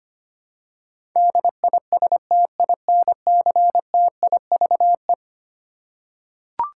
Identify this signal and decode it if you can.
Morse code, heard as a keyed tone, about 25 wpm, 700 Hz. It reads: DISTINCTIVE
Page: morse-code-ninja-practice-25wpm